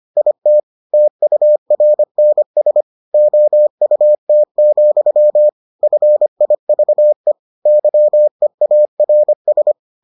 Morse code, heard as a keyed tone, about 25 wpm, 600 Hz. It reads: IT TURNS OUT, FIVE YEARS